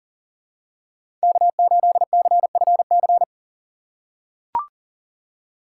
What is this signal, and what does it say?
Morse code, heard as a keyed tone, about 37 wpm, 700 Hz. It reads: K8CFC
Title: USA Call Signs - 5 Characters - 2x Character Spacing - 40wpm